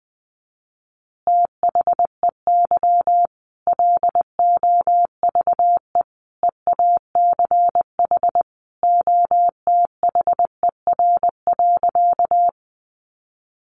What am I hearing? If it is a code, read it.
Morse code, heard as a keyed tone, about 20 wpm, 700 Hz. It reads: THEY LOVE EACH OTHER.
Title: Easy Phrases 20wpm